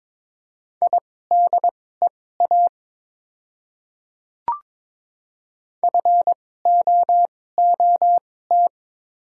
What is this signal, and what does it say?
Morse code, heard as a keyed tone, about 22 wpm, spaced as for 16 wpm, 700 Hz. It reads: IDEA FOOT
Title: 2x Character Spacing - Top 500 Words - 22wpm